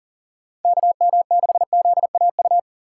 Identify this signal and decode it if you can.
Morse code, heard as a keyed tone, about 48 wpm, 700 Hz. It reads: KM6ZAU
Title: Speed Racing - USA 2x3 Call Signs 40wpm